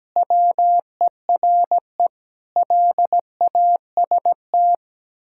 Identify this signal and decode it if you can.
Morse code, heard as a keyed tone, about 17 wpm, 700 Hz. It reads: WERE LAST
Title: Pangram Challenge v2 - 17wpm